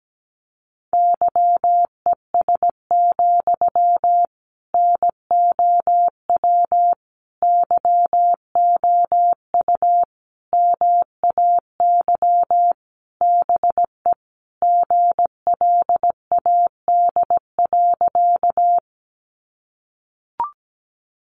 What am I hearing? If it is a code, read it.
Morse code, heard as a keyed tone, about 17 wpm, 700 Hz. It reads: YES, NOW YOU MAY BE GLAD.
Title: Sentences from Top 800 Words 17wpm